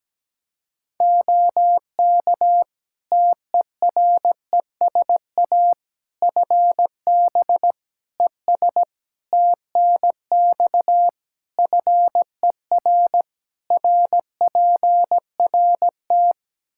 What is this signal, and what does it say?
Morse code, heard as a keyed tone, about 17 wpm, 700 Hz. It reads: OK TERESA FB ES TNX FER RPRT
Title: Practice QSO - Exchange 2 - Info - 17wpm